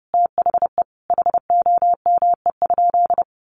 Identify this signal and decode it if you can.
Morse code, heard as a keyed tone, about 30 wpm, 700 Hz. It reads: THE HOME?
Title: Sentences from Top 1300 Words 30wpm